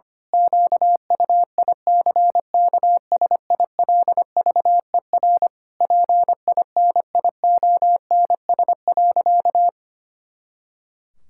Morse code, heard as a keyed tone, about 25 wpm, 700 Hz. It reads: QUICKSILVER PINIONS.
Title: Pangram Challenge v2 - 25wpm